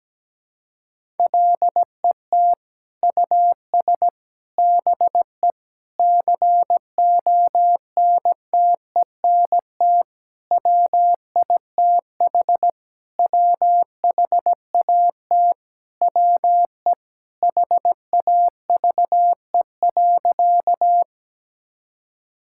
Morse code, heard as a keyed tone, about 17 wpm, 700 Hz. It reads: LET US BE CONTENT WITH WHAT WE HAVE.
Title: Sentences from Top 1000 Words - Review Part 6 - 17wpm